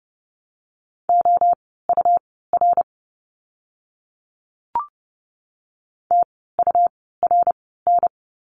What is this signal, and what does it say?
Morse code, heard as a keyed tone, about 30 wpm, spaced as for 17 wpm, 700 Hz. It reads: OUR TURN
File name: course / 30:10wpm / 115 Lesson 19f - Letter w - Words 30wpm